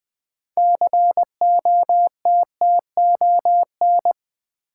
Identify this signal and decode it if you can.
Morse code, heard as a keyed tone, about 20 wpm, 700 Hz. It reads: COTTON
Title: Build A Word - 3 Letters At A Time - 6-Letter Words - 20wpm